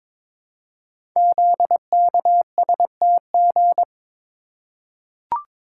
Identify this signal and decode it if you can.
Morse code, heard as a keyed tone, about 22 wpm, 700 Hz. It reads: ZKSTG
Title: Code Groups - 5 Letters - 22wpm